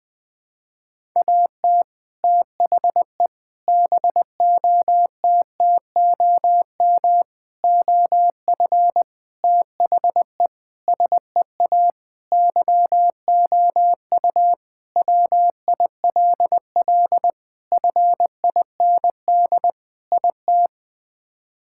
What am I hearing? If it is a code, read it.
Morse code, heard as a keyed tone, about 21 wpm, 700 Hz. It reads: AT THE BOTTOM OF THE SEA YOU WILL FIND IT
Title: Sentences with 3-Word Phrases - 20wpm